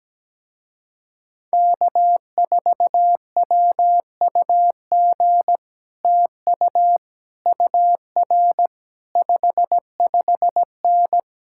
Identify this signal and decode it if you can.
Morse code, heard as a keyed tone, about 17 wpm, 700 Hz. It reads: K4WUG TU UR 55N
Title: Speed-Racing Common QSO Phrases Encore - 17wpm